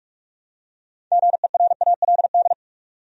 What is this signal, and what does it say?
Morse code, heard as a keyed tone, about 46 wpm, 700 Hz. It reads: GERALD
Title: USA Common Names - 45wpm